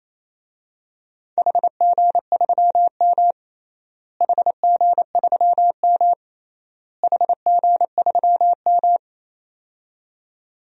Morse code, heard as a keyed tone, about 28 wpm, 700 Hz. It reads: HG3M HG3M HG3M
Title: DXCC-100 Call Signs - 4 Characters - Repeated 3x - 28wpm